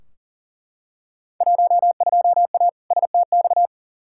Morse code, heard as a keyed tone, about 40 wpm, 700 Hz. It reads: 12A STX
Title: Field Day Exchanges Encore - 40wpm